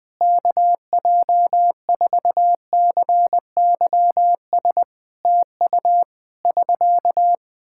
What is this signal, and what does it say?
Morse code, heard as a keyed tone, about 21 wpm, 700 Hz. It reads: KJ4CYS TU <SK>
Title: Practice QSO - Exchange 4 - Ending - 20wpm